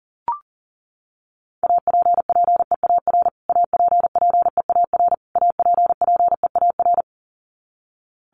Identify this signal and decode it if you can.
Morse code, heard as a keyed tone, about 37 wpm, 700 Hz. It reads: APPEAR APPEAR APPEAR
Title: Top 500 Words Repeated 40wpm